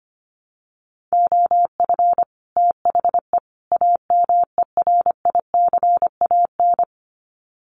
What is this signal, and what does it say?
Morse code, heard as a keyed tone, about 25 wpm, 700 Hz. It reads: OF THE AMERICAN
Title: Sets of 3 Words 25wpm